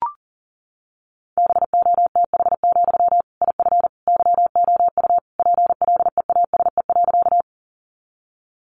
Morse code, heard as a keyed tone, about 40 wpm, 700 Hz. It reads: BOTH, IF YOU PLEASE.
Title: Sentences from Top 1500 Words 40wpm